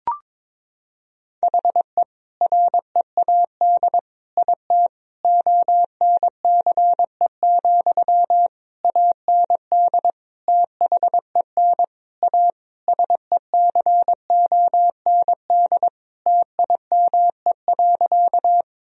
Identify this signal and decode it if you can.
Morse code, heard as a keyed tone, about 22 wpm, 700 Hz. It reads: HE READ IT ONCE, AND THEN A SECOND TIME.
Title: Sentences from Top 1000 Words - Review Part 1 - 22wpm